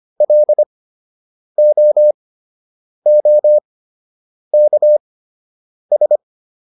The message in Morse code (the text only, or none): LOOKS